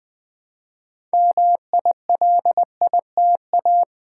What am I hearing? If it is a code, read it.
Morse code, heard as a keyed tone, about 20 wpm, 700 Hz. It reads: MILITA
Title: Build A Word - 2 Letters At A Time - 8-Letter Words - 20wpm